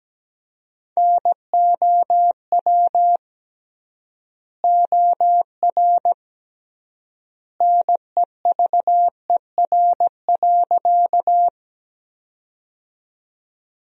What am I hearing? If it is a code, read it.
Morse code, heard as a keyed tone, about 17 wpm, 700 Hz. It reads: NOW OR NEVER.
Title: 3x Word-Spacing - Sentences from Top 300 Words - 17wpm